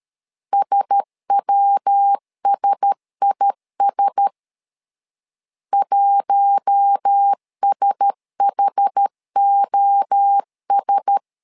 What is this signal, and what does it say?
Morse code, heard as a keyed tone, about 13 wpm, 800 Hz. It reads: SWSIS 1SHOS
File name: LEZIONI / 4 Q1S 12wpm 1802